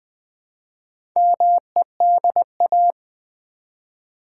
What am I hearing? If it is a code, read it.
Morse code, heard as a keyed tone, about 20 wpm, 700 Hz. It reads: MEDA
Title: Build A Word - 1 Letter At A Time - 5-Letter Words - 20wpm